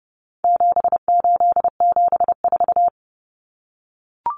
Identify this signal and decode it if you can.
Morse code, heard as a keyed tone, about 30 wpm, 700 Hz. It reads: 7874
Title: Common Cut Numbers - 4-Digit 30wpm